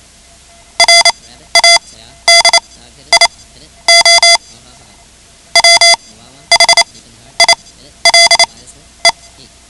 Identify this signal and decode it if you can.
Morse code, heard as a keyed tone, about 28 wpm, spaced as for 12 wpm, 2050 Hz. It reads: RADIO WHILE
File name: ham radio / CodeQuick / Code Quick 10